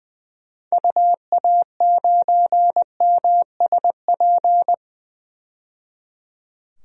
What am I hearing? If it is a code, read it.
Morse code, heard as a keyed tone, about 20 wpm, 700 Hz. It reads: UA9MSP
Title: DXCC-100 Call Signs - 6 Characters - 20wpm